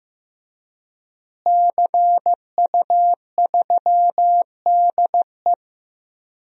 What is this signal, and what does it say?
Morse code, heard as a keyed tone, about 15 wpm, 700 Hz. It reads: CU3DE
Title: DXCC-100 Call Signs - 5 Characters - Repeated 3x - 15wpm